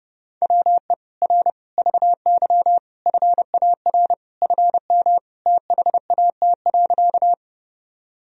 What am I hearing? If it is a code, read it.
Morse code, heard as a keyed tone, about 30 wpm, 700 Hz. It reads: WE R VY FAR FM THAT.
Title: Sentences from Top 300 Words and Abbreviations 30wpm